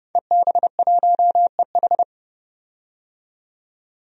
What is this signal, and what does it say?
Morse code, heard as a keyed tone, about 30 wpm, 700 Hz. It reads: EB1EH
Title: DXCC-100 Call Signs - 5 Characters - 30wpm